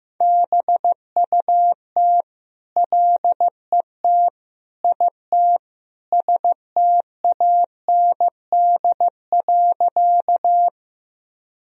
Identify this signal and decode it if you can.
Morse code, heard as a keyed tone, about 15 wpm, 700 Hz. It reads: BUT LET IT STAND.
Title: Sentences from Top 1000 Words - Review Part 1 - 15wpm